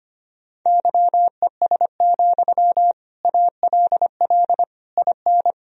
Morse code, heard as a keyed tone, about 25 wpm, 700 Hz. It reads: YES, ALL IN
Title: Sentences from Top 1000 Words - Review Part 1 - 25wpm